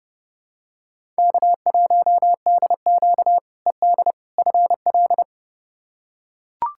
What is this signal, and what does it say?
Morse code, heard as a keyed tone, about 30 wpm, 700 Hz. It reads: K1DQ ED FL
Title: K1USN SST QSO Practice - 30wpm